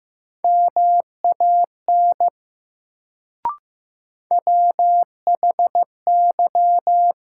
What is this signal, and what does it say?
Morse code, heard as a keyed tone, about 15 wpm, 700 Hz. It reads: MAN WHY
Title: Sentences from Top 200 Words - New Word Drill - 15wpm